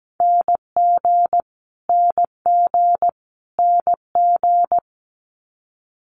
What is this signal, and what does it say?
Morse code, heard as a keyed tone, about 17 wpm, 700 Hz. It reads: NG NG NG
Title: Multi-letter Phonemes Repeated 3x 17wpm